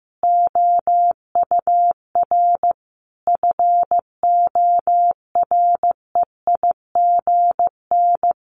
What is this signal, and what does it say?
Morse code, heard as a keyed tone, about 15 wpm, 700 Hz. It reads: OUR FOREIGN
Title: Sentences from Top 1400 Words 15wpm